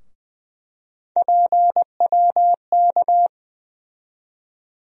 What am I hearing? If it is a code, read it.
Morse code, heard as a keyed tone, about 20 wpm, 700 Hz. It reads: PWK
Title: USA Call Sign Postfixes - 3 Letters 20wpm